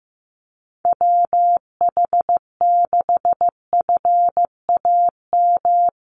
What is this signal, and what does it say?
Morse code, heard as a keyed tone, about 15 wpm, 700 Hz. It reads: WH6FAM
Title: USA Call Signs 15wpm